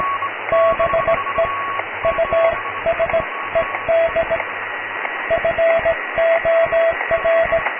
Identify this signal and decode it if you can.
Morse code, heard as a keyed tone, about 18 wpm, spaced as for 15 wpm, 660 Hz, with radio noise. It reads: BE USED FOR
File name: cw / tekst(b)+QRM